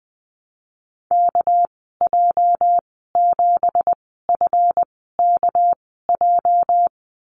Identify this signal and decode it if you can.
Morse code, heard as a keyed tone, about 19 wpm, spaced as for 14 wpm, 700 Hz. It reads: KJ7FKJ
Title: 231 Lesson 37g - Letter x - Call Signs 20wpm